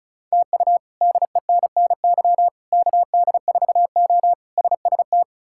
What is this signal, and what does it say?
Morse code, heard as a keyed tone, about 34 wpm, 700 Hz. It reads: TU DENNY KD4O SST
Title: K1USN SST QSO Practice - 35wpm